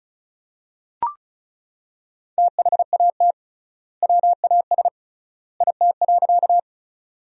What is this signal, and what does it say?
Morse code, heard as a keyed tone, about 34 wpm, 700 Hz. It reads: THAT WAS IT.
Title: 3x Word-Spacing - Sentences from Top 100 Words - 35wpm